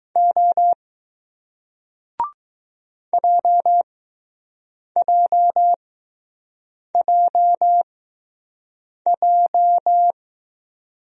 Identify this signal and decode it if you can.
Morse code, heard as a keyed tone, about 20 wpm, spaced as for 6 wpm, 700 Hz. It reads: O JJJJ